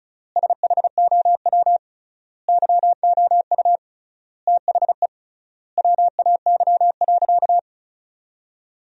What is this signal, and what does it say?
Morse code, heard as a keyed tone, about 36 wpm, 700 Hz. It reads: SHOW YOU THE WAY.
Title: 3x Word-Spacing - Sentences from Top 300 Words - 35wpm